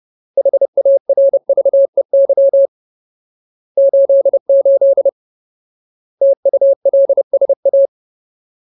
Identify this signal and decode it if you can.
Morse code, heard as a keyed tone, about 30 wpm, 550 Hz. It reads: HARVEY 88 TULSA
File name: Ultimate Word List / 30WPM Ultimate